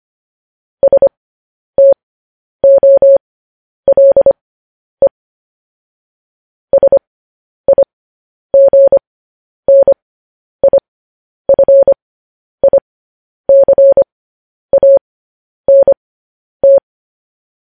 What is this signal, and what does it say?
Morse code, heard as a keyed tone, about 25 wpm, spaced as for 10 wpm, 550 Hz. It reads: STOLE SIGNIFICANT
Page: news-headlines-in-morse-code-at-10-wpm